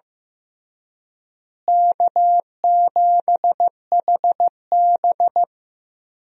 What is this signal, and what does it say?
Morse code, heard as a keyed tone, about 15 wpm, 700 Hz. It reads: K7HB
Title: USA Call Signs - 4 Characters - Repeated 3x - 15wpm